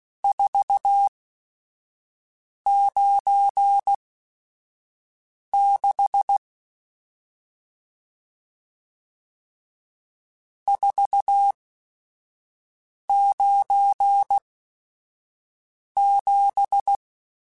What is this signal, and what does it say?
Morse code, heard as a keyed tone, about 16 wpm, spaced as for 5 wpm, 800 Hz. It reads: 496 497
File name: RSTs / RST-mp3s / Various RST @5